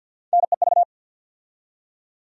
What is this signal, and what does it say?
Morse code, heard as a keyed tone, about 53 wpm, 700 Hz. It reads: NEV
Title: USA Call Sign Postfixes - 3 Letters 50wpm